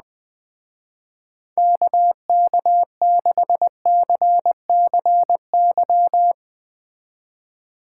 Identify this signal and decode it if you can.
Morse code, heard as a keyed tone, about 20 wpm, 700 Hz. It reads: KK6CCY
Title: Speed Racing - USA 2x3 Call Signs 17wpm